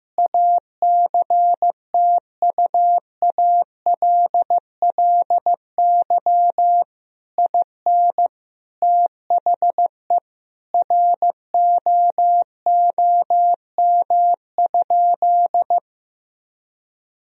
Morse code, heard as a keyed tone, about 15 wpm, 700 Hz. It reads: ACTUALLY IN THE ROOM?
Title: Sentences from Top 1000 Words - Review Part 4 - 15wpm